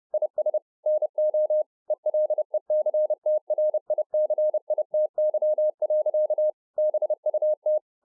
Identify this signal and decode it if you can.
Morse code, heard as a keyed tone, about 30 wpm, 610 Hz. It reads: IS NO ELECTRICITY. BUT